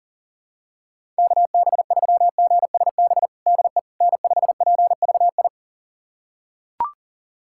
Morse code, heard as a keyed tone, about 37 wpm, 700 Hz. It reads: KB3GSB DE N5PVI
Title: USA Callsign DE Callsign 40wpm